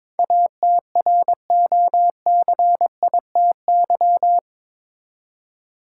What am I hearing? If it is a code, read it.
Morse code, heard as a keyed tone, about 22 wpm, 700 Hz. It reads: ATROCITY
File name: practice / 22wpm / Eight Letter Words - 22wpm